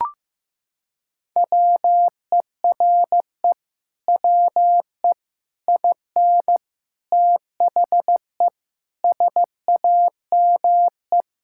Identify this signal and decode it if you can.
Morse code, heard as a keyed tone, about 15 wpm, 700 Hz. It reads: WERE WE IN THE SAME
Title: Sentences from Top 1000 Words - Review Part 6 - 15wpm